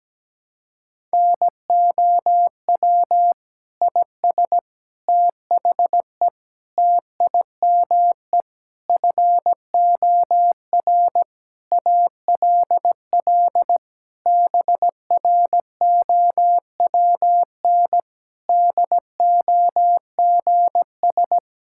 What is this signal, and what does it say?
Morse code, heard as a keyed tone, about 17 wpm, 700 Hz. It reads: NOW IS THE TIME FOR ALL BROWN DOGS
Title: Pangram Challenge v2 - 17wpm